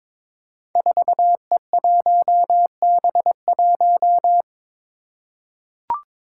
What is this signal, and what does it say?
Morse code, heard as a keyed tone, about 22 wpm, 700 Hz. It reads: 4E1B1
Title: Five Challenging Letter-Number Combinations 22wpm